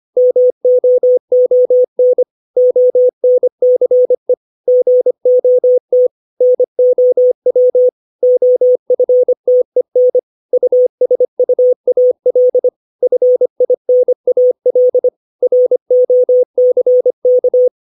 Morse code, heard as a keyed tone, about 25 wpm, 500 Hz. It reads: MOON ONCE GOT NOW OFTEN USUAL FINAL ROCK
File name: 632 words / 632 Words 25WPM